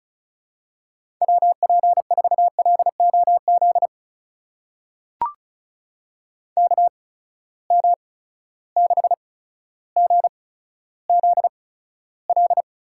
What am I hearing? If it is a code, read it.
Morse code, heard as a keyed tone, about 36 wpm, spaced as for 15 wpm, 700 Hz. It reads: WP4LOZ KM6GZL
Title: USA Call Signs - 6 Characters - 8x Character Spacing - 35wpm